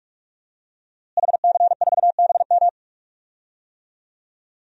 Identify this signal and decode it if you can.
Morse code, heard as a keyed tone, about 46 wpm, 700 Hz. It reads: HC4BM
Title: DXCC-100 Call Signs - 5 Characters - Repeated 3x - 45wpm